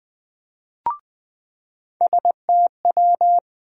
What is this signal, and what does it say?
Morse code, heard as a keyed tone, about 20 wpm, 700 Hz. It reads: STW
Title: USA Call Sign Postfixes - 3 Letters 20wpm